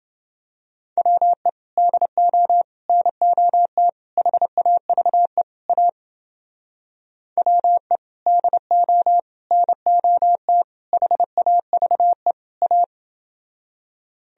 Morse code, heard as a keyed tone, about 29 wpm, 700 Hz. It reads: WE DO NOT HAVE A WE DO NOT HAVE A